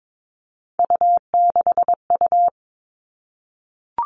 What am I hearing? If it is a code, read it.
Morse code, heard as a keyed tone, about 22 wpm, 700 Hz. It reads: U6U
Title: All Cut Numbers - 3-Digit 22wpm